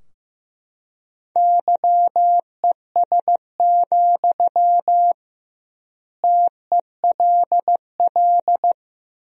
Morse code, heard as a keyed tone, about 15 wpm, 700 Hz. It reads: YES, TELL
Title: 2x Word-Spacing - Sentences from Top 500 Words - 15wpm